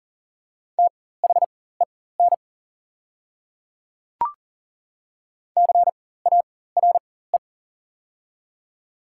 Morse code, heard as a keyed tone, about 40 wpm, spaced as for 19 wpm, 690 Hz. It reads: THEN CARE
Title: 4x Character Spacing - Top 300 Words - 40wpm